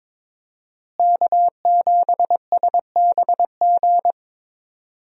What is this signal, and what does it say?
Morse code, heard as a keyed tone, about 22 wpm, 700 Hz. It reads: K7SBG
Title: USA Call Signs - 5 Characters - Repeated 2x - 22wpm